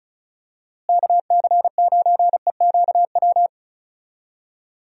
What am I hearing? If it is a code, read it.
Morse code, heard as a keyed tone, about 36 wpm, 700 Hz. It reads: KC9EQW
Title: USA Call Signs - 6 Characters - Repeated 2x - 35wpm